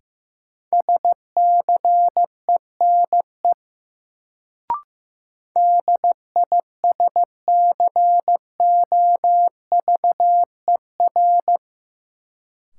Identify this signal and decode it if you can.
Morse code, heard as a keyed tone, about 15 wpm, 700 Hz. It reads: SCENE DISCOVER
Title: Sentences from Top 1100 Words - New Word Drill - 15wpm